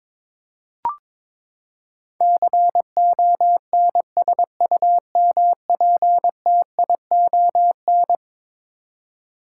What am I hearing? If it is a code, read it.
Morse code, heard as a keyed tone, about 21 wpm, 700 Hz. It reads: CONSUMPTION
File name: practice / 22wpm / Eleven Letter Words - 22wpm